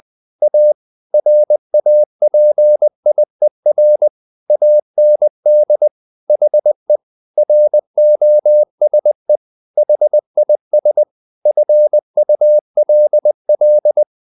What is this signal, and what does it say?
Morse code, heard as a keyed tone, about 20 wpm, 600 Hz. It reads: A RAPIER AND HE ROSE HIS FULL
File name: the old man and the sea / the old man and the sea part 1 20wpm